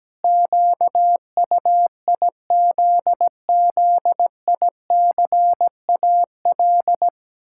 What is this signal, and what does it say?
Morse code, heard as a keyed tone, about 17 wpm, 700 Hz. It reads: QUIZZICAL
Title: Pangram Challenge v2 - 17wpm